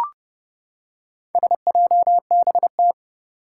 Morse code, heard as a keyed tone, about 28 wpm, 700 Hz. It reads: SJBT
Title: Four Challenging Letter-Number Combinations 30wpm